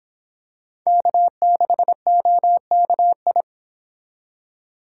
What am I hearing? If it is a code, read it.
Morse code, heard as a keyed tone, about 25 wpm, 700 Hz. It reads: K6OKI